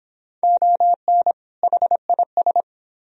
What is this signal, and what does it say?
Morse code, heard as a keyed tone, about 29 wpm, 700 Hz. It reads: ON HIS